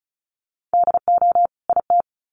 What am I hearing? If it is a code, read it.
Morse code, heard as a keyed tone, about 34 wpm, 700 Hz. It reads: DO IT
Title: Sets of 2 Words 35wpm